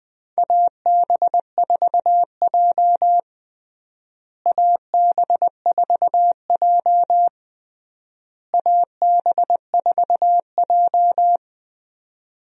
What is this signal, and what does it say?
Morse code, heard as a keyed tone, about 20 wpm, 700 Hz. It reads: AB4J AB4J AB4J
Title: USA Call Signs - 4 Characters - Repeated 3x - 20wpm